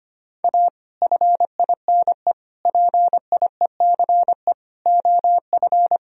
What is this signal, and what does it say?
Morse code, heard as a keyed tone, about 25 wpm, 700 Hz. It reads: A FINE PIECE OF
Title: Sentences from Top 1000 Words - Review Part 2 - 25wpm